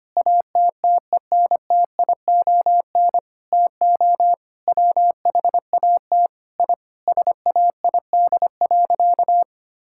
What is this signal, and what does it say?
Morse code, heard as a keyed tone, about 25 wpm, 700 Hz. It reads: ATTENTION TO WHAT I SAID.
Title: Sentences from Top 1000 Words - Review Part 6 - 25wpm